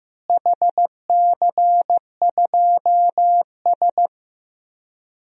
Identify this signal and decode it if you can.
Morse code, heard as a keyed tone, about 15 wpm, 700 Hz. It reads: HC2S
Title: DXCC-100 Call Signs - 4 Characters - Repeated 2x - 15wpm